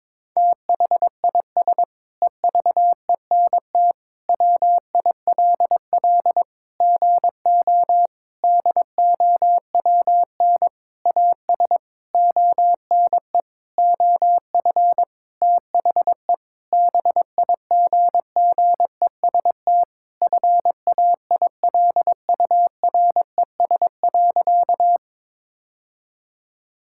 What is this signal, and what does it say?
Morse code, heard as a keyed tone, about 22 wpm, 700 Hz. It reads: THIS EVENT WILL GO DOWN AS ONE OF THE BIGGEST FAILURES.
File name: Sentences with 7-Word Phrases - 22wpm